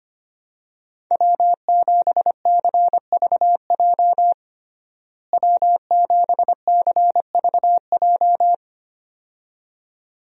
Morse code, heard as a keyed tone, about 25 wpm, 700 Hz. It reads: W7CVJ W7CVJ
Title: USA Call Signs - 5 Characters - Repeated 2x - 25wpm